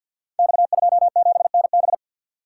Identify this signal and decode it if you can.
Morse code, heard as a keyed tone, about 46 wpm, 700 Hz. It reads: X27NB
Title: Code Groups - 5 Letter:Numbers - 50wpm